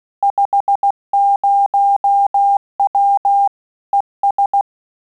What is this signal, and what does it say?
Morse code, heard as a keyed tone, about 16 wpm, 800 Hz. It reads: 50W ES
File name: Telegraphy-group / QSO CW 15 PPM